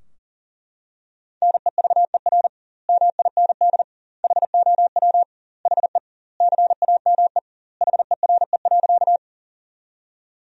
Morse code, heard as a keyed tone, about 40 wpm, 700 Hz. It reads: NEVER MIND HOW HE CAME HERE.
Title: 2x Word-Spacing - Sentences from Top 500 Words - 40wpm